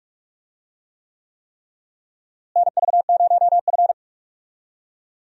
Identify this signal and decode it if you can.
Morse code, heard as a keyed tone, about 43 wpm, 700 Hz. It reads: NV0F
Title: USA Call Signs - 4 Characters - Repeated 3x - 45wpm